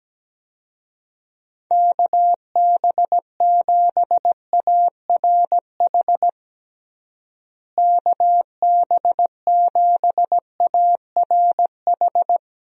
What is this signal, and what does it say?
Morse code, heard as a keyed tone, about 17 wpm, 700 Hz. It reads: KB7ARH KB7ARH
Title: USA Call Signs - 6 Characters - Repeated 3x - 17wpm